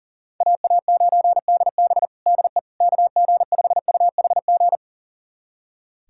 Morse code, heard as a keyed tone, about 40 wpm, 700 Hz. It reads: AA9DB DE KG5UHG
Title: USA Callsign DE Callsign 40wpm